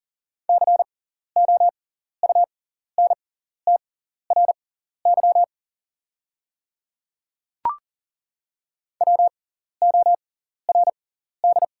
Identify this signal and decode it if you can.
Morse code, heard as a keyed tone, about 40 wpm, spaced as for 14 wpm, 690 Hz. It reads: COUNTRY WORD
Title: 6x Character Spacing - Top 300 Words - 40wpm